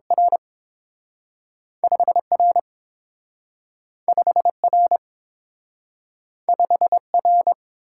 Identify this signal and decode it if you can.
Morse code, heard as a keyed tone, about 30 wpm, 700 Hz. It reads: R 5 R 5 R 5 R